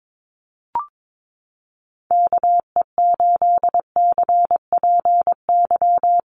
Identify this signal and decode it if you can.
Morse code, heard as a keyed tone, about 22 wpm, 700 Hz. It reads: KE8CPY
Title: USA Call Signs Repeated 3x 22wpm